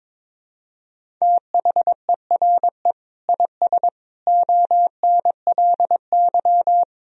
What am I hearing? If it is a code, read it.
Morse code, heard as a keyed tone, about 22 wpm, 700 Hz. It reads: THERE IS ONLY
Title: Sentences from Top 300 Words and Abbreviations 22wpm